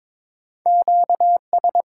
Morse code, heard as a keyed tone, about 22 wpm, 690 Hz. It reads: QS
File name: Top 90 QSO Elements - 8x Character Spacing - 22wpm